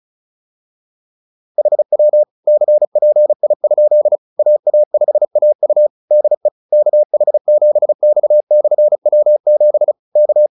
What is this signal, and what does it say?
Morse code, heard as a keyed tone, about 34 wpm, 600 Hz. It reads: HW CPI? AA5AU DE KH7X/W7 K